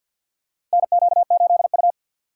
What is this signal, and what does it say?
Morse code, heard as a keyed tone, about 50 wpm, 700 Hz. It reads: NQ8U
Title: USA Call Signs - 4 Characters - 2x Character Spacing - 50wpm